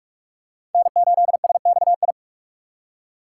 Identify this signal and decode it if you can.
Morse code, heard as a keyed tone, about 44 wpm, 700 Hz. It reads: N8SXI